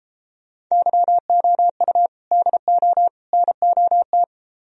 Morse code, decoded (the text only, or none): YOU DO NOT